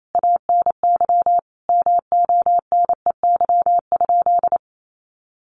Morse code, heard as a keyed tone, about 28 wpm, 700 Hz. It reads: ANY MONEY?
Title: Sentences from Top 400 Words 28wpm